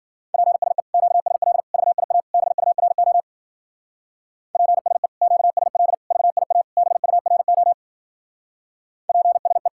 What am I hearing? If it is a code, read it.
Morse code, heard as a keyed tone, about 52 wpm, 700 Hz. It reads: PSE QSL VIA BURO PSE QSL VIA BURO PSE